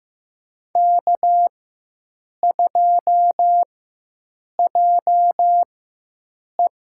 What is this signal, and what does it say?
Morse code, heard as a keyed tone, about 15 wpm, spaced as for 7 wpm, 700 Hz. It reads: K2JE
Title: USA Call Signs - 4x Character Spacing - 15wpm